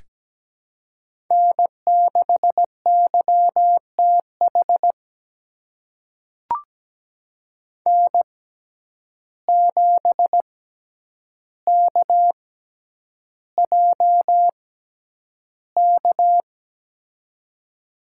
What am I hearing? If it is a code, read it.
Morse code, heard as a keyed tone, about 17 wpm, spaced as for 9 wpm, 700 Hz. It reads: N6YTH N7KJK
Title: USA Call Signs - 5 Characters - 6x Character Spacing - 17wpm